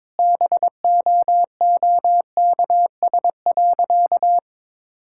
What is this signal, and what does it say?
Morse code, heard as a keyed tone, about 21 wpm, 700 Hz. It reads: BOOKS.
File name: practice / 22wpm / Sentences with 3-Word Phrases - 22wpm